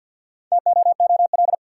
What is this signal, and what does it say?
Morse code, heard as a keyed tone, about 52 wpm, 700 Hz. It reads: TOOL
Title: Sentences from Top 1000 Words - Review Part 6 - New Word Drill - 50wpm